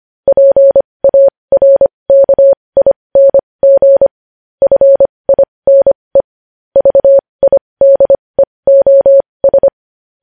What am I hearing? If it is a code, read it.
Morse code, heard as a keyed tone, about 25 wpm, spaced as for 20 wpm, 550 Hz. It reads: PARKING FINE VIDEOS
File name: podcast / News headlines-1x-25x20